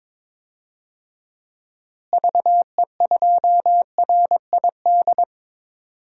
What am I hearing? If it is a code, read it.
Morse code, heard as a keyed tone, about 22 wpm, 700 Hz. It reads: VE2RID
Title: DXCC-100 Call Signs - 6 Characters - Repeated 3x - 22wpm